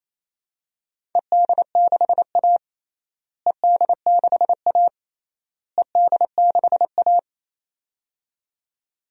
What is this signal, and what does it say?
Morse code, heard as a keyed tone, about 27 wpm, 700 Hz. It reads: ED6A ED6A ED6A
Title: DXCC-100 Call Signs - 4 Characters - Repeated 3x - 28wpm